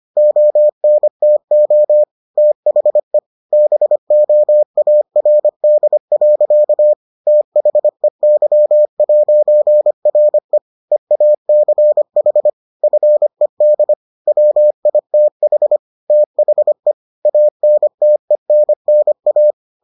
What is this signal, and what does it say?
Morse code, heard as a keyed tone, about 25 wpm, 600 Hz. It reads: ONTO THE BOARD. THEY'RE EACH FED WITH THE ANTENNA